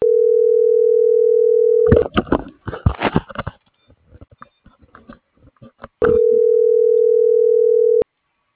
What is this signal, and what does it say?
Ringback tone (North America), 440 + 480 Hz.